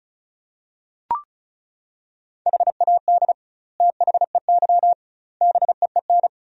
Morse code, heard as a keyed tone, about 36 wpm, 700 Hz. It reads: HAD THEY BEEN
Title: 2x Word-Spacing - Sentences from Top 400 Words - 35wpm